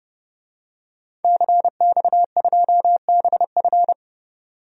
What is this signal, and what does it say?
Morse code, heard as a keyed tone, about 31 wpm, 700 Hz. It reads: CX2BF
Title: DXCC-100 Call Signs - 5 Characters - 30wpm